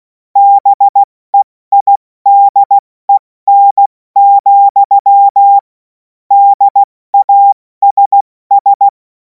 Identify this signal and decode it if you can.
Morse code, heard as a keyed tone, about 16 wpm, 800 Hz. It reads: BEIDEN, DASS
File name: Morsekurs / Witze / Witz 07. April 2026 70 F80